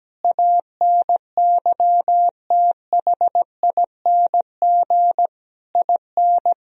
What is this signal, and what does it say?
Morse code, heard as a keyed tone, about 17 wpm, 700 Hz. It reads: ANYTHING IN
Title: Sentences from Top 1000 Words - Review Part 2 - 17wpm